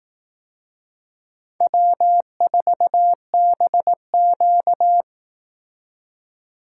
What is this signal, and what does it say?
Morse code, heard as a keyed tone, about 18 wpm, 700 Hz. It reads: W4BQ
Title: Speed Racing - USA 1x2 Call Signs 15wpm